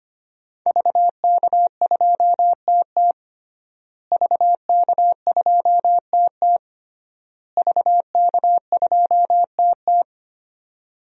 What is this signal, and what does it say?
Morse code, heard as a keyed tone, about 25 wpm, 700 Hz. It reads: VK2TT VK2TT VK2TT
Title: DXCC-100 Call Signs - 5 Characters - Repeated 3x - 25wpm